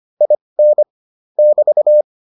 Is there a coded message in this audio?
Yes: Morse code from 0.2 to 2.0 s, heard as a keyed tone, about 25 wpm, spaced as for 20 wpm, 600 Hz. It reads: IN <BT>